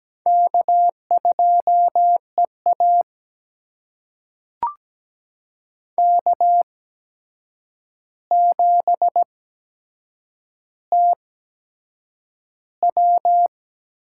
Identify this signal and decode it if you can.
Morse code, heard as a keyed tone, about 17 wpm, spaced as for 7 wpm, 700 Hz. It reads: K2EA K7TW
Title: USA Call Signs - 4 Characters - 8x Character Spacing - 17wpm